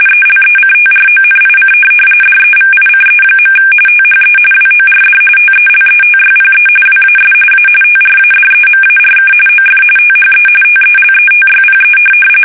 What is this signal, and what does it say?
RTTY, about 45.45 baud. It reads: ELECTRONIC DEVICE. THE WORD TELETYPE WAS A TRADEMARK OF THE TELETYPE
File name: fsk850 / teletype-RTTY